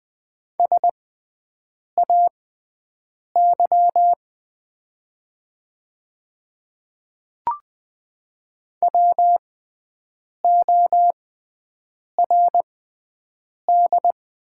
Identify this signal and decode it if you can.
Morse code, heard as a keyed tone, about 20 wpm, spaced as for 7 wpm, 700 Hz. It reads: SAY WORD
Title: 6x Character Spacing - Top 200 Words - 20wpm